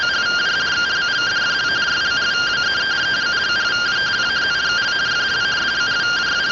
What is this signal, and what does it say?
Two-tone teleprinter signal, about 170 Hz shift.